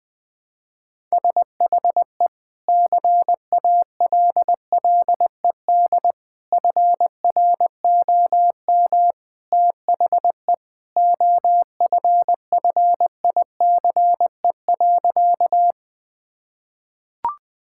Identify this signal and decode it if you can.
Morse code, heard as a keyed tone, about 20 wpm, 700 Hz. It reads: SHE CALLED FROM THE OFFICE.
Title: Sentences with 2-Word Phrases - 20wpm